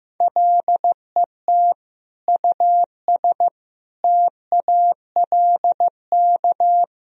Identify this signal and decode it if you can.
Morse code, heard as a keyed tone, about 15 wpm, 700 Hz. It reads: LET US TALK
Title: Sentences from Top 400 Words and Abbreviations 15wpm